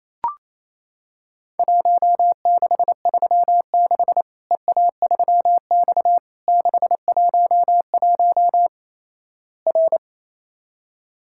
Morse code, heard as a keyed tone, about 28 wpm, 700 Hz. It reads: 1636 EA3X 611 R
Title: Worked All Europe DX Contest - QTC Exchange - 28wpm